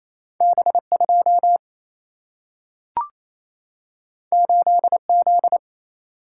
Morse code, heard as a keyed tone, about 28 wpm, 700 Hz. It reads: B2 8Z
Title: Single-Letter Single-Number 28wpm